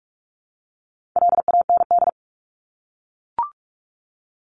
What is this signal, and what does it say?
Morse code, heard as a keyed tone, about 44 wpm, 700 Hz. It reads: LAND
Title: Top 200 Words 45wpm